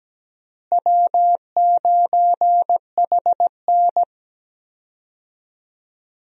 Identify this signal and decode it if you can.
Morse code, heard as a keyed tone, about 17 wpm, 700 Hz. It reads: W9HN
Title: USA Call Signs - 4 Characters - Repeated 3x - 17wpm